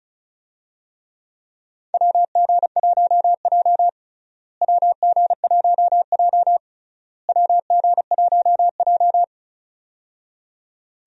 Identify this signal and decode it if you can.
Morse code, heard as a keyed tone, about 36 wpm, 700 Hz. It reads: WG1J WG1J WG1J
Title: USA Call Signs - 4 Characters - Repeated 3x - 35wpm